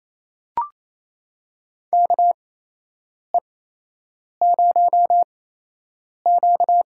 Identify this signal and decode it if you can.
Morse code, heard as a keyed tone, about 28 wpm, spaced as for 8 wpm, 700 Hz. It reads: KE0Q
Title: USA Call Signs - 6 Characters - 8x Character Spacing - 28wpm